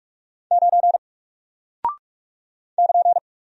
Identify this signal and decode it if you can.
Morse code, heard as a keyed tone, about 44 wpm, 700 Hz. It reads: 9 <KN>
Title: Single Character (Letter:Number:Punctuation:Prosign) Alt 45wpm